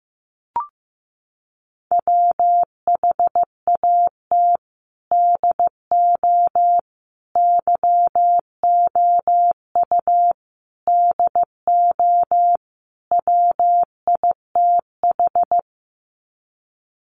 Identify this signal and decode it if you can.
Morse code, heard as a keyed tone, about 15 wpm, 700 Hz. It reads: WHAT DO YOU DO WITH
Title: Sets of 5 Words Encore2 15wpm